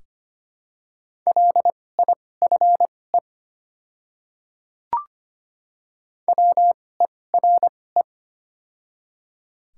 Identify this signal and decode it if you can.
Morse code, heard as a keyed tone, about 25 wpm, spaced as for 18 wpm, 700 Hz. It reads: LIFE WERE
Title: 2x Character Spacing - Top 500 Words - 25wpm